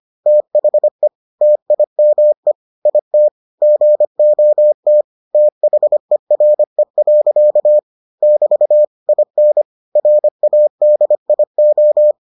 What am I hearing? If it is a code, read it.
Morse code, heard as a keyed tone, about 25 wpm, 600 Hz. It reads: THE TIME IT GOT THERE. <BT> IN RADIO